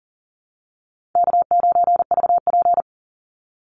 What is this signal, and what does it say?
Morse code, heard as a keyed tone, about 39 wpm, 700 Hz. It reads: K9VP
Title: Call Signs Repeated 2x - 40wpm